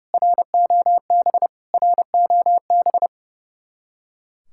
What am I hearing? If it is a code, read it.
Morse code, heard as a keyed tone, about 37 wpm, 700 Hz. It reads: ROB ROB
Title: Speed-Racing Common QSO Phrases - 30wpm